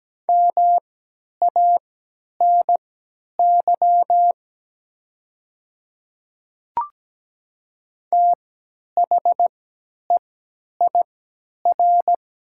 Morse code, heard as a keyed tone, about 17 wpm, spaced as for 10 wpm, 700 Hz. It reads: MANY THEIR
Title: 3x Character Spacing - Top 100 Words - 17wpm